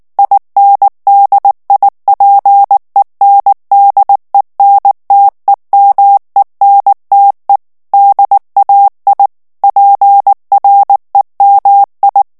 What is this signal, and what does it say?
Morse code, heard as a keyed tone, about 19 wpm, 800 Hz. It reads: INDIPENDENTEMENTE DAI PREMI
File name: morse mp3 / ITALIEN texte clair 2 vitesse 18 mots minute